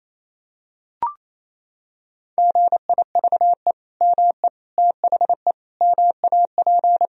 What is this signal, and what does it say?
Morse code, heard as a keyed tone, about 28 wpm, 700 Hz. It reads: GIVE ME THE MAP
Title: Sentences from Top 1000 Words - Review Part 2 - 28wpm